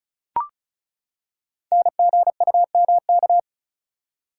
Morse code, heard as a keyed tone, about 36 wpm, 700 Hz. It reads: NGUMK
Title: Code Groups - 5 Letters - 35wpm